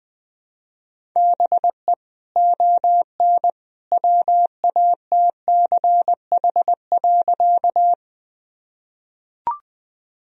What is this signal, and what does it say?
Morse code, heard as a keyed tone, about 20 wpm, 700 Hz. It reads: BE ON WATCH.
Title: Sentences from Top 1000 Words - Review Part 1 - 20wpm